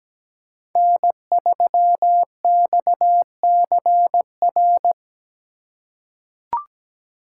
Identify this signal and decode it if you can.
Morse code, heard as a keyed tone, about 17 wpm, 700 Hz. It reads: N3XCR
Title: USA Call Signs - 5 Characters - 2x Character Spacing - 17wpm